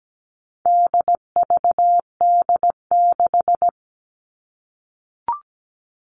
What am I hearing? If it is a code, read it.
Morse code, heard as a keyed tone, about 17 wpm, 700 Hz. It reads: DVD6
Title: All Cut Numbers - 4-Digit 17wpm